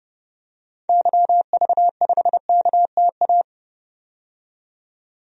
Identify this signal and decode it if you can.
Morse code, heard as a keyed tone, about 30 wpm, 700 Hz. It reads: YV5KTA
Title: DXCC-100 Call Signs - 6 Characters - 30wpm